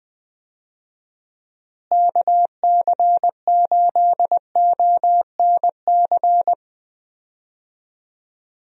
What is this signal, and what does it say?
Morse code, heard as a keyed tone, about 20 wpm, 700 Hz. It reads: KC8ONC
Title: USA Call Signs - 6 Characters - Repeated 3x - 20wpm